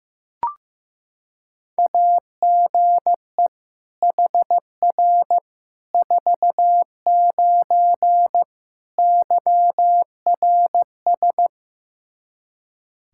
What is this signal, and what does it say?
Morse code, heard as a keyed tone, about 15 wpm, 700 Hz. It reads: AGE HR 49 YRS
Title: Practice QSO - Exchange 3 - More Info - 15wpm